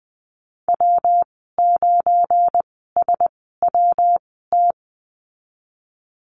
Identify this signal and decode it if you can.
Morse code, heard as a keyed tone, about 20 wpm, spaced as for 15 wpm, 700 Hz. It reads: W9SWT
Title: 173 Lesson 28g - Number 9 - Call Signs 20wpm